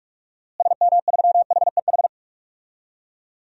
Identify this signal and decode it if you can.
Morse code, heard as a keyed tone, about 44 wpm, 700 Hz. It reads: SM3HEH
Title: DXCC-100 Call Signs - 6 Characters - Repeated 2x - 45wpm